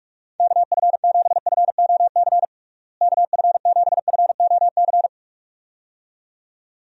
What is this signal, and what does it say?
Morse code, heard as a keyed tone, about 46 wpm, 700 Hz. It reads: KF7FOC KF7FOC
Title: USA Call Signs - 6 Characters - Repeated 2x - 45wpm